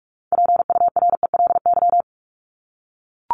Morse code, heard as a keyed tone, about 44 wpm, 700 Hz. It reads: PURELY
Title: LY Suffix - Top 100 Words with Base Words 45wpm